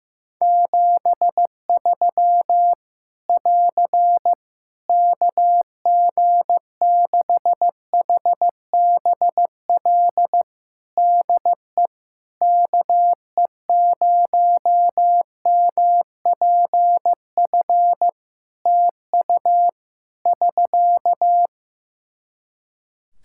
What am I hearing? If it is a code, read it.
Morse code, heard as a keyed tone, about 15 wpm, 700 Hz. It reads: 73 <AR> KG6HBL DE KE0MPF TU <SK>
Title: Practice QSO - Exchange 4 - Ending - 15wpm